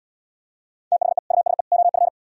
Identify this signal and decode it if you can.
Morse code, heard as a keyed tone, about 77 wpm, 700 Hz. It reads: THE USE OF